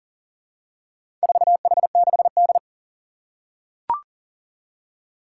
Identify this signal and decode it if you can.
Morse code, heard as a keyed tone, about 37 wpm, 700 Hz. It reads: 4H6D
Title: Four Challenging Letter-Number Combinations 40wpm